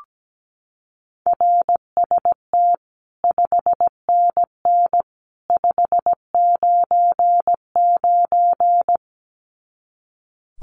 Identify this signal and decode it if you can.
Morse code, heard as a keyed tone, about 17 wpm, 700 Hz. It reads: RST 5NN 599
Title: Common QSO Phrases 17wpm